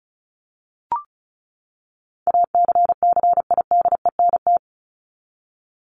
Morse code, heard as a keyed tone, about 34 wpm, 700 Hz. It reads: ACCIDENT
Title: Top 1751-2000 Words 35wpm